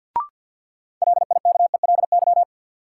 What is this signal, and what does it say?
Morse code, heard as a keyed tone, about 46 wpm, 700 Hz. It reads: LIKELY
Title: Sentences from Top 600 Words - New Word Drill - 50wpm